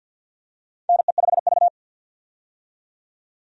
Morse code, heard as a keyed tone, about 52 wpm, 700 Hz. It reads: NE5V
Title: USA Call Signs - 4 Characters - 50wpm